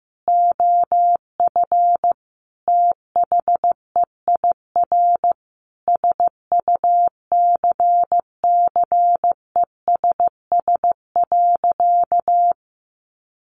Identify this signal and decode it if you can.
Morse code, heard as a keyed tone, about 15 wpm, 700 Hz. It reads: OF THEIR SUCCESS.
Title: Sentences from Top 900 Words 15wpm